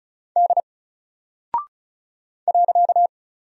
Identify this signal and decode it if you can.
Morse code, heard as a keyed tone, about 36 wpm, 700 Hz. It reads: D .
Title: Single Character (Letter:Number:Punctuation:Prosign) Alt 35wpm